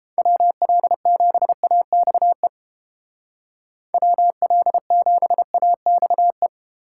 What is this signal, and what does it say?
Morse code, heard as a keyed tone, about 31 wpm, 700 Hz. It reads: WL7AXE WL7AXE